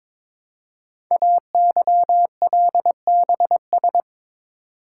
Morse code, heard as a keyed tone, about 23 wpm, 700 Hz. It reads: AYLBS
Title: Code Groups - 5 Letter:Numbers - 22wpm